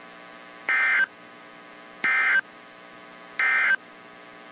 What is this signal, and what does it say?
Emergency Alert System (SAME) tones, heard as harsh buzzing bursts. It is heard from 0.6 to 3.8 s.
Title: WZ2523 - NOAA Weather Radio